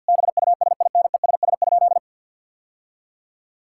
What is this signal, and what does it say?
Morse code, heard as a keyed tone, about 50 wpm, 700 Hz. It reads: BUSINESS?
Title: Sentences from Top 1600 Words 50wpm